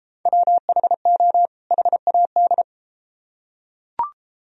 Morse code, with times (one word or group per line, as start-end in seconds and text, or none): 0.3-1.5 WHO
1.7-2.6 HAD